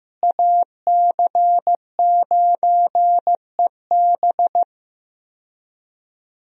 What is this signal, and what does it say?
Morse code, heard as a keyed tone, about 15 wpm, 700 Hz. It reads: AC9EB
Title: USA Call Signs - 5 Characters - Repeated 3x - 15wpm